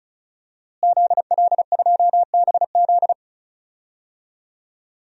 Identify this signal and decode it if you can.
Morse code, heard as a keyed tone, about 34 wpm, 700 Hz. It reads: ZL2BZ
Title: DXCC-100 Call Signs - 5 Characters - Repeated 3x - 35wpm